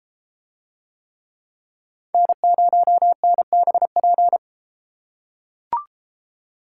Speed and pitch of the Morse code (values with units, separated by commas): 33 wpm, 700 Hz